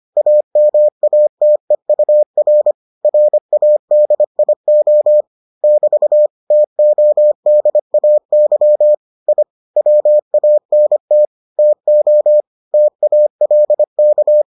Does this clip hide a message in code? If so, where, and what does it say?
Yes: Morse code from 0.2 to 14.4 s, heard as a keyed tone, about 25 wpm, 600 Hz. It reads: AMATEUR RADIO <BT> TODAY I WANT TO TALK